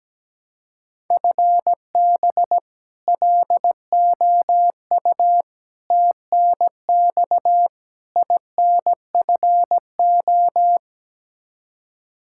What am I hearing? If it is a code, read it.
Morse code, heard as a keyed tone, about 17 wpm, 700 Hz. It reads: FB LOU TNX INFO
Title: K1USN SST QSO Practice 17wpm